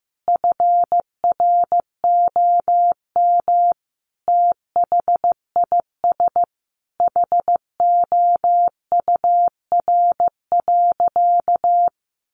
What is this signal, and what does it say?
Morse code, heard as a keyed tone, about 15 wpm, 700 Hz. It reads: FROM THIS HOUR.
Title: Sentences from Top 1100 Words 15wpm